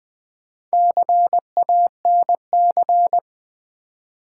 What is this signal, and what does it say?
Morse code, heard as a keyed tone, about 20 wpm, 700 Hz. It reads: CANC
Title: Build A Word - 2 Letters At A Time - 6-Letter Words - 20wpm